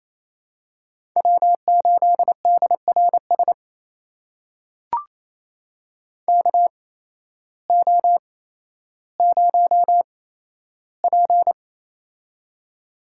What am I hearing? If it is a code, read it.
Morse code, heard as a keyed tone, about 28 wpm, spaced as for 13 wpm, 700 Hz. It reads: W8DRS KO0P
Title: USA Call Signs - 8x Character Spacing - 28wpm